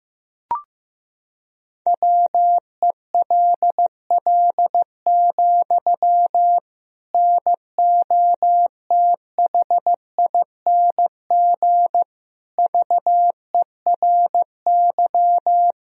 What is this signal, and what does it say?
Morse code, heard as a keyed tone, about 15 wpm, 700 Hz. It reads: WELL, NOTHING VERY
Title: Sentences from Top 1000 Words - Review Part 3 - 15wpm